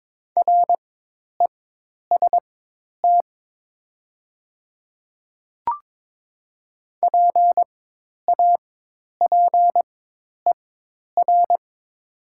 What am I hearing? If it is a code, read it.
Morse code, heard as a keyed tone, about 22 wpm, spaced as for 10 wpm, 700 Hz. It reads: REST PAPER
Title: 4x Character Spacing - Top 500 Words - 22wpm